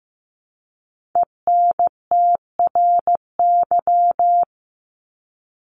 Morse code, heard as a keyed tone, about 15 wpm, 700 Hz. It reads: ENTRY
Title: Top 1251-1500 Words 15wpm